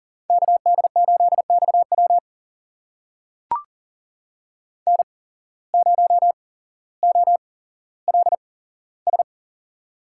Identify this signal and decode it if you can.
Morse code, heard as a keyed tone, about 40 wpm, spaced as for 17 wpm, 700 Hz. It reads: KD8XW N0OLS
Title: USA Call Signs - 5 Characters - 8x Character Spacing - 40wpm